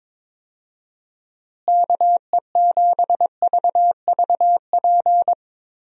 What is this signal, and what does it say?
Morse code, heard as a keyed tone, about 22 wpm, 700 Hz. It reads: KE7VVP
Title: USA Call Signs - 6 Characters - Repeated 3x - 22wpm